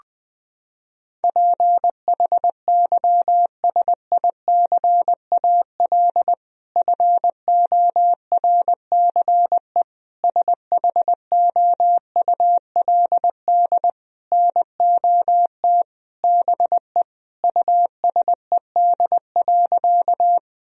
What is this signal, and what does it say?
Morse code, heard as a keyed tone, about 21 wpm, 700 Hz. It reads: PHYSICAL FORCE SHOULD NOT BE USED.
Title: Sentences from Top 1000 Words - Review Part 4 - 20wpm